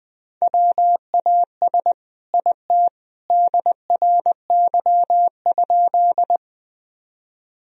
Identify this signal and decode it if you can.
Morse code, heard as a keyed tone, about 20 wpm, 700 Hz. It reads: WAS IT DRY?
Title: Sentences from Top 1000 Words - Review Part 3 - 20wpm